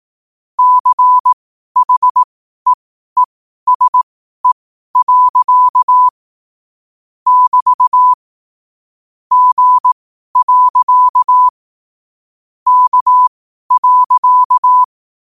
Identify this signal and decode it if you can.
Morse code, heard as a keyed tone, about 18 wpm, spaced as for 12 wpm, 1000 Hz. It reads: CHEESE. <BT> G. K.